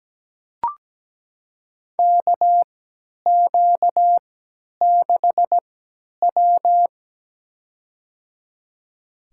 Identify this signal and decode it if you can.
Morse code, heard as a keyed tone, about 17 wpm, spaced as for 10 wpm, 700 Hz. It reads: KQ6W
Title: USA Call Signs - 4 Characters - 3x Character Spacing - 17wpm